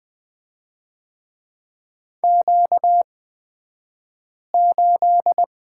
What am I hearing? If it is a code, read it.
Morse code, heard as a keyed tone, about 20 wpm, spaced as for 5 wpm, 700 Hz. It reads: Q8
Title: 235 Lesson 38d - Letter q - Two Characters 20wpm